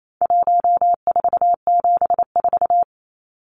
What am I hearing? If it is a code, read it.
Morse code, heard as a keyed tone, about 28 wpm, 700 Hz. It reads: 1474
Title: Numbers - 4-Digit 28wpm